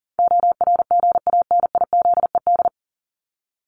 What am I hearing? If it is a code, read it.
Morse code, heard as a keyed tone, about 40 wpm, 700 Hz. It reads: ORGANIZED
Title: Top 1001-1250 Words 40wpm